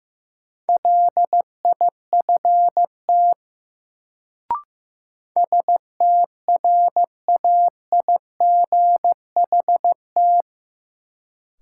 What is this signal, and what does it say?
Morse code, heard as a keyed tone, about 15 wpm, 700 Hz. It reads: LIFT STRAIGHT
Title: Sentences from Top 1000 Words - New Word Drill - 15wpm